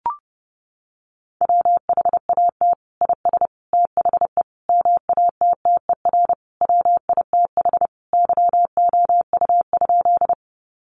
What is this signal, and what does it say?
Morse code, heard as a keyed tone, about 30 wpm, 700 Hz. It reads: WHAT IS THE MATTER WITH YOU?
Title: Sentences from Top 700 Words 30wpm